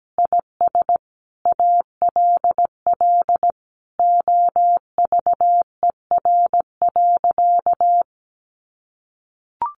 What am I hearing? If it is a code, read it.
Morse code, heard as a keyed tone, about 17 wpm, 700 Hz. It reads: IS ALL OVER.
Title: Sentences from Top 1300 Words 17wpm